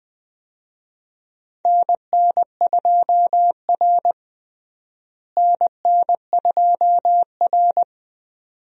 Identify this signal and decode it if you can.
Morse code, heard as a keyed tone, about 20 wpm, 700 Hz. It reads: NN2R NN2R
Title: USA Call Signs - 4 Characters - Repeated 2x - 20wpm